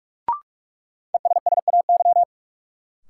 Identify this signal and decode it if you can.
Morse code, heard as a keyed tone, about 44 wpm, 700 Hz. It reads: ESSAY
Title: Sentences from Top 1500 Words - New Word Drill - 45wpm